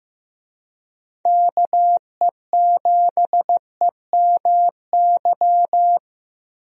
Morse code, heard as a keyed tone, about 15 wpm, 700 Hz. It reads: KE7EMY
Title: USA Call Signs - 6 Characters - Repeated 3x - 15wpm